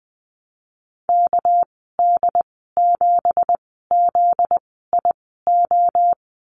Morse code, heard as a keyed tone, about 20 wpm, spaced as for 14 wpm, 700 Hz. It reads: KD7ZIO
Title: 245 Lesson 39g - Letter z - Call Signs 20wpm